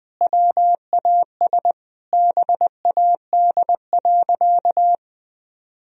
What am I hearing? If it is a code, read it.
Morse code, heard as a keyed tone, about 20 wpm, 700 Hz. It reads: WAS BAD.
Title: Sentences from Top 1000 Words - Review Part 1 - 20wpm